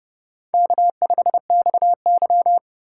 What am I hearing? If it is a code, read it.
Morse code, heard as a keyed tone, about 32 wpm, 700 Hz. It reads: K5XY
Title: USA Call Signs - 4 Characters - 3x Character Spacing - 30wpm